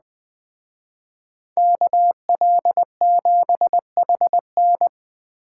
Morse code, heard as a keyed tone, about 20 wpm, 700 Hz. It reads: KL7HN